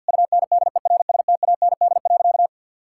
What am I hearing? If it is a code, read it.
Morse code, heard as a keyed tone, about 50 wpm, 700 Hz. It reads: UNDERSTAND.
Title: Sentences from Top 1000 Words - Review Part 6 - 50wpm